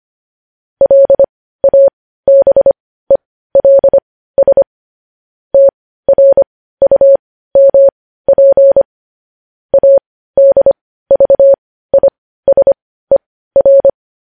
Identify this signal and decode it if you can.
Morse code, heard as a keyed tone, about 25 wpm, spaced as for 15 wpm, 550 Hz. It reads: LABELS TRUMP ADVISER
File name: podcast / News headlines-1x-25x15